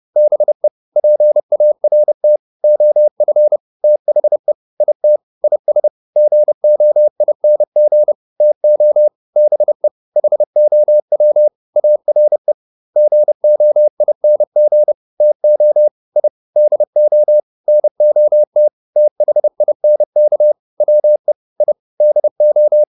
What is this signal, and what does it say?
Morse code, heard as a keyed tone, about 30 wpm, 600 Hz. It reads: BE PART OF THE IT IS GOING TO BE HOW ARE GOING TO I DO NOT THINK WE I DO